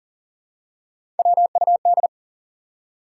Morse code, heard as a keyed tone, about 40 wpm, 700 Hz. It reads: WUD
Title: Top 180 QSO Elements 40wpm